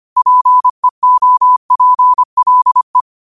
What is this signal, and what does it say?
Morse code, heard as a keyed tone, about 25 wpm, 1000 Hz. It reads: PEOPLE
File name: qotdfiles / 25wpm